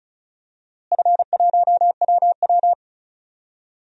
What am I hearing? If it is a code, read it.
Morse code, heard as a keyed tone, about 36 wpm, 700 Hz. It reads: F1WW
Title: DXCC-100 Call Signs - 4 Characters - Repeated 3x - 35wpm